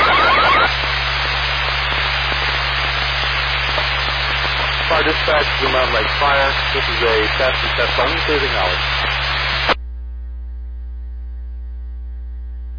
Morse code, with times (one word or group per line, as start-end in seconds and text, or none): none